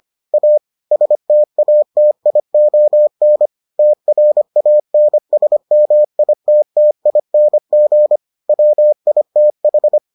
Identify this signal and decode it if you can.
Morse code, heard as a keyed tone, about 25 wpm, 600 Hz. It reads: A STATION TRANSMITTING WITH